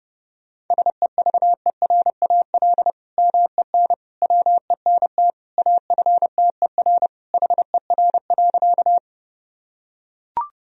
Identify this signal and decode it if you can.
Morse code, heard as a keyed tone, about 32 wpm, 700 Hz. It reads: SEVERAL MEN WENT AFTER HER.
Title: Sentences from Top 1000 Words - Review Part 2 - 30wpm